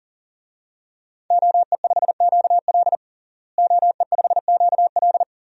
Morse code, heard as a keyed tone, about 40 wpm, 690 Hz. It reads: OE5QL OE5QL
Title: DXCC-100 Call Signs - 5 Characters - Repeated 2x - 40wpm